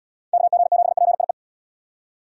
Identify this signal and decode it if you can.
Morse code, heard as a keyed tone, about 76 wpm, 700 Hz. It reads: KG7PIE